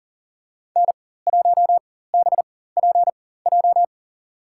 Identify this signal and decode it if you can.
Morse code, heard as a keyed tone, about 40 wpm, spaced as for 19 wpm, 700 Hz. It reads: N1BPJ
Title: USA Call Signs - 5 Characters - 4x Character Spacing - 40wpm